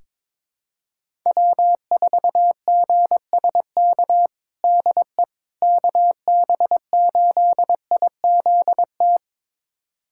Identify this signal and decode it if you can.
Morse code, heard as a keyed tone, about 22 wpm, 700 Hz. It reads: W4GSK DE KB8IZT
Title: USA Callsign DE Callsign 22wpm